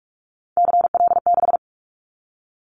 Morse code, heard as a keyed tone, about 46 wpm, 700 Hz. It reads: CL6
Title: Callsign Prefixes for Top 100 DXCC - 45wpm